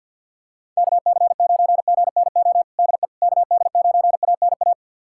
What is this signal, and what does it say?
Morse code, heard as a keyed tone, about 51 wpm, 700 Hz. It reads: KC9GNO DE KD9ANA
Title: USA Callsign DE Callsign - 6 Characters - 50wpm